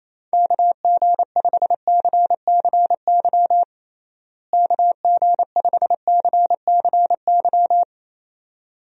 Morse code, heard as a keyed tone, about 28 wpm, 700 Hz. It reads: KG5CCY KG5CCY
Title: USA Call Signs - 6 Characters - Repeated 2x - 28wpm